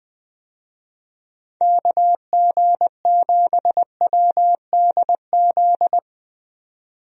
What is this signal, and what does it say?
Morse code, heard as a keyed tone, about 21 wpm, 700 Hz. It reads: KG7WDZ
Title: USA Call Signs - 6 Characters - 20wpm